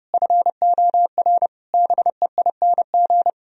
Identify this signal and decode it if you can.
Morse code, heard as a keyed tone, about 40 wpm, 700 Hz. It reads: FOR BEING